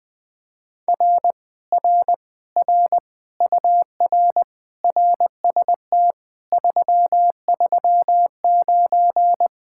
Morse code, heard as a keyed tone, about 20 wpm, 700 Hz. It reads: R R R UR RST 339
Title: Speed-Racing Common QSO Phrases - 20wpm